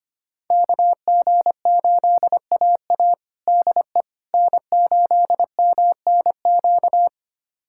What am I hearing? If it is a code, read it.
Morse code, heard as a keyed tone, about 25 wpm, 700 Hz. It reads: KG8AA DE N8MNQ
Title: USA Callsign DE Callsign - 5 Characters - 25wpm